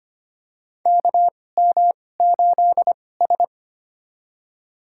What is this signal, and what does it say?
Morse code, heard as a keyed tone, about 25 wpm, spaced as for 18 wpm, 700 Hz. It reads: KM8S
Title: USA Call Signs - 4 Characters - 2x Character Spacing - 25wpm